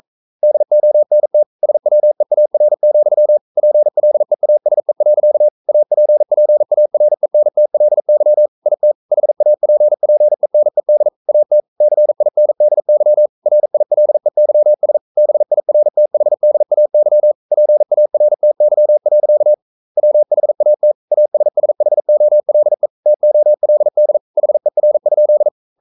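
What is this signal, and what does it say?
Morse code, heard as a keyed tone, about 42 wpm, 600 Hz. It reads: DONT SWEAR, PLEASE. APPARENTLY IT HAPPENED AT CINDY RILEYS BIRTHDAY PARTY. WHAT ASSHOLE TOLD HER?
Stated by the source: Morse code sending of an English novel at 42 wpm.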